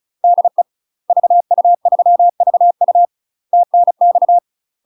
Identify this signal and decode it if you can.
Morse code, heard as a keyed tone, about 34 wpm, 700 Hz. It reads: DE VU3VU TNX
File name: UK MASTER QSOs TRAINING / 35WPM / UK MASTER QSOs 0000